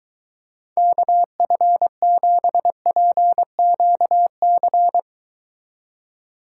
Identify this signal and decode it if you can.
Morse code, heard as a keyed tone, about 23 wpm, 700 Hz. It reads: KF7PQC